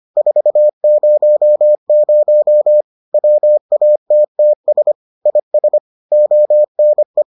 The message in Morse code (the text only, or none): 400 WATTS IS ONE